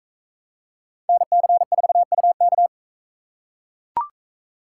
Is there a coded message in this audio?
Yes: Morse code from 1.1 to 2.7 s, heard as a keyed tone, about 44 wpm, 700 Hz. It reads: NC4UK